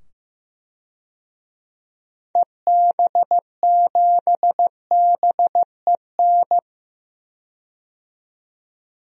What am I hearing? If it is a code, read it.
Morse code, heard as a keyed tone, about 15 wpm, 700 Hz. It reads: EB7BEN
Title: DXCC-100 Call Signs - 6 Characters - 15wpm